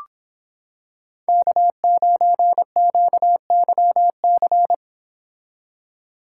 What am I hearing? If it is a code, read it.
Morse code, heard as a keyed tone, about 26 wpm, 700 Hz. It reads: K9QYC